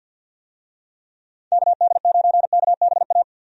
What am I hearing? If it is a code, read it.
Morse code, heard as a keyed tone, about 48 wpm, 700 Hz. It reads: KD9KBA
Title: USA Call Signs - 6 Characters - 50wpm